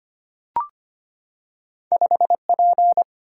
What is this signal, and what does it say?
Morse code, heard as a keyed tone, about 25 wpm, 700 Hz. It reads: 5P
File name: Two Letter-Number Combinations - 25wpm